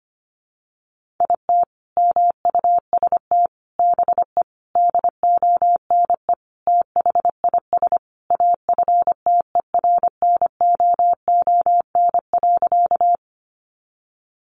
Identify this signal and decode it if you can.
Morse code, heard as a keyed tone, about 25 wpm, 700 Hz. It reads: IT MUST BE DONE THIS AFTERNOON.
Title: Sentences from Top 1500 Words 25wpm